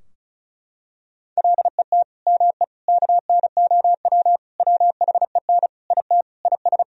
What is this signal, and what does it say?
Morse code, heard as a keyed tone, about 36 wpm, 700 Hz. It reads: LET ME KNOW WHEN IT IS
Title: Sentences from Top 1000 Words - Review Part 2 - 35wpm